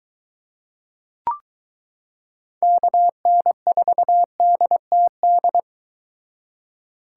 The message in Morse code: KN4DTD

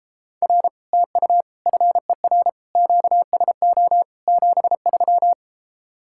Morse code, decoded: R TU FER QSO 73

